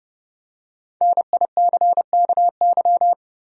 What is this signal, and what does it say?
Morse code, heard as a keyed tone, about 30 wpm, 700 Hz. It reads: NICKY
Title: USA Common Names - 30wpm